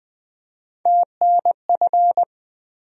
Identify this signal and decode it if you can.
Morse code, heard as a keyed tone, about 20 wpm, 700 Hz. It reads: TNF
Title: USA Call Sign Postfixes - 3 Letters 20wpm